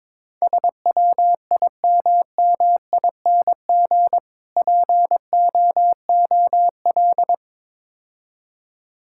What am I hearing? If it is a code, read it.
Morse code, heard as a keyed tone, about 22 wpm, 700 Hz. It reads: SWIMMING POOL
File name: Easy Words - Speed-Racing with Hints - 22wpm